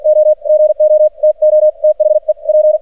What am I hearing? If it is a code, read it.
Morse code, heard as a keyed tone, about 49 wpm, spaced as for 43 wpm, 610 Hz. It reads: O GO TO THE P
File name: archived / CRSnet